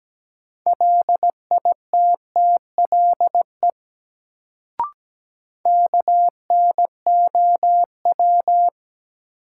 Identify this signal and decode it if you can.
Morse code, heard as a keyed tone, about 17 wpm, 700 Hz. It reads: LITTLE KNOW
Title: Sets of 2 Words - New Word Drill - 17wpm